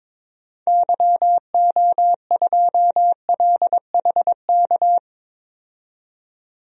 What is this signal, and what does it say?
Morse code, heard as a keyed tone, about 22 wpm, 700 Hz. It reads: YO2LHK
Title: DXCC-100 Call Signs - 6 Characters - 22wpm